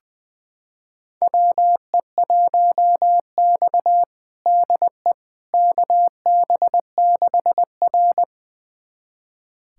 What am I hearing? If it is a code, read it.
Morse code, heard as a keyed tone, about 21 wpm, 700 Hz. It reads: WE1X DE KB6R
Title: USA Callsign DE Callsign - 4 Characters - 20wpm